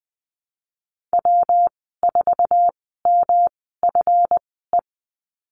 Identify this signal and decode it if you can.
Morse code, heard as a keyed tone, about 20 wpm, spaced as for 15 wpm, 700 Hz. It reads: W4MFE
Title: 140 Lesson 23g - Letter f - Call Signs 20wpm